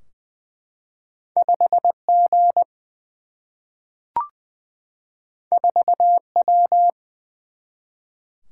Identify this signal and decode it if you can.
Morse code, heard as a keyed tone, about 20 wpm, 700 Hz. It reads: 5G 4W
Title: Single-Letter Single-Number 20wpm